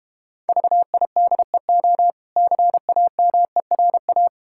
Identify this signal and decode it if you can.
Morse code, heard as a keyed tone, about 32 wpm, 700 Hz. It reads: VIDEO CAMERA